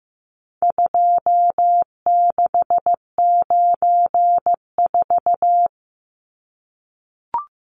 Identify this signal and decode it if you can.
Morse code, heard as a keyed tone, about 15 wpm, 700 Hz. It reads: 2694
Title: Numbers - 4-Digit 15wpm